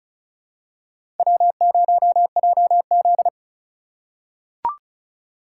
Morse code, heard as a keyed tone, about 34 wpm, 700 Hz. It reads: W0JZ
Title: USA Call Signs - 4 Characters - 2x Character Spacing - 35wpm